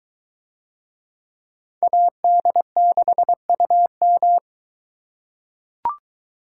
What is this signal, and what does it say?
Morse code, heard as a keyed tone, about 21 wpm, 700 Hz. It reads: AD6UM